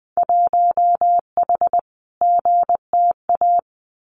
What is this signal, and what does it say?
Morse code, heard as a keyed tone, about 20 wpm, 700 Hz. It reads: 1H GTA
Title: Winter Field Day Exchanges 20wpm